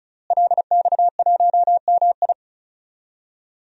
Morse code, heard as a keyed tone, about 36 wpm, 700 Hz. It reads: LX1MI
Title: DXCC-100 Call Signs - 5 Characters - Repeated 2x - 35wpm